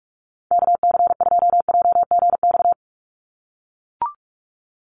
Morse code, heard as a keyed tone, about 44 wpm, 700 Hz. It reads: KC2JGX
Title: USA Call Signs 45wpm